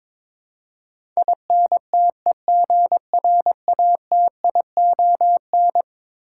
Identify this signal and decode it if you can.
Morse code, heard as a keyed tone, about 22 wpm, 700 Hz. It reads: INTEGRATION
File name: practice / 22wpm / Eleven Letter Words - 22wpm